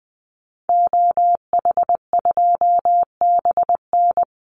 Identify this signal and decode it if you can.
Morse code, heard as a keyed tone, about 20 wpm, 700 Hz. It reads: OH2BN
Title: Call Signs Repeated 2x - 20wpm